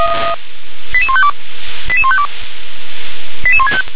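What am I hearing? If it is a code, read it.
Five-tone selective call (ZVEI), heard as quick runs of five tones, tones of about 80 ms. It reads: 88152 88152 88165
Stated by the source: ZVEI five-tone call 88101 received on SDR.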